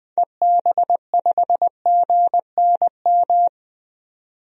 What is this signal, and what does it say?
Morse code, heard as a keyed tone, about 20 wpm, 700 Hz. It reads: EB5GNM
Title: DXCC-100 Call Signs - 6 Characters - 20wpm